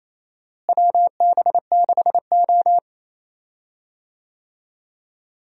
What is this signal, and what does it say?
Morse code, heard as a keyed tone, about 28 wpm, 700 Hz. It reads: WB6O
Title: USA Call Signs - 4 Characters - Repeated 2x - 28wpm